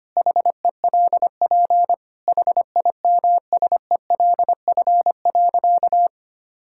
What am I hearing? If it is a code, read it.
Morse code, heard as a keyed tone, about 25 wpm, 700 Hz. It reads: HELP HIMSELF.
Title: Sentences from Top 1600 Words 25wpm